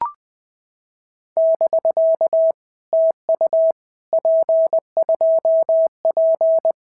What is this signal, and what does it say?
Morse code, heard as a keyed tone, about 19 wpm, 650 Hz. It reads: <BK> TU P2P
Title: POTA QSO Practice - 20wpm